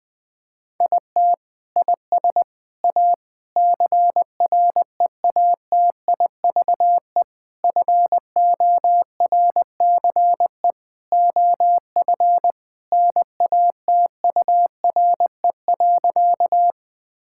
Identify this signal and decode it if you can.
Morse code, heard as a keyed tone, about 20 wpm, 700 Hz. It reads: IT IS A CREATIVE FORCE OF NATURE.
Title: Sentences from Top 1000 Words - Review Part 5 - 20wpm